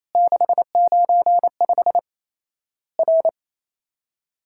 Morse code, heard as a keyed tone, about 28 wpm, 700 Hz. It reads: 695 R
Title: Worked All Europe DX Contest - QTC Exchange - 28wpm